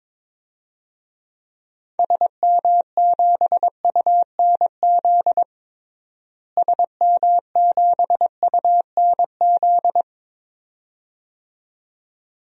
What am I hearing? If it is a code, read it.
Morse code, heard as a keyed tone, about 22 wpm, 700 Hz. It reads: SM7UNZ SM7UNZ
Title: DXCC-100 Call Signs - 6 Characters - Repeated 2x - 22wpm